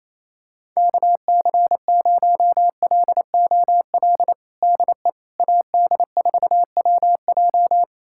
Morse code, heard as a keyed tone, about 27 wpm, 700 Hz. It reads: KC0LOL DE AD4WJ
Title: USA Callsign DE Callsign 28wpm